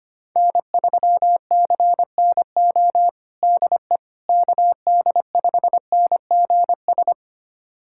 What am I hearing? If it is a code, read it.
Morse code, heard as a keyed tone, about 25 wpm, 700 Hz. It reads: N3CNO DE KD5NGS
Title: USA Callsign DE Callsign 25wpm